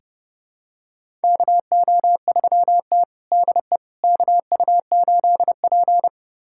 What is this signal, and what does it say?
Morse code, heard as a keyed tone, about 32 wpm, 700 Hz. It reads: KO3T DE KU8P
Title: USA Callsign DE Callsign - 4 Characters - 30wpm